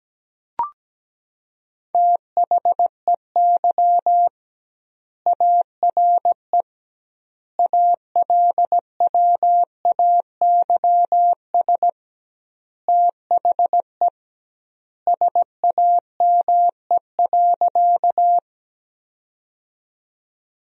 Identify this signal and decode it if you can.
Morse code, heard as a keyed tone, about 17 wpm, 700 Hz. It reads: THEY ARE ALWAYS THE SAME.
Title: 2x Word-Spacing - Sentences from Top 300 Words - 17wpm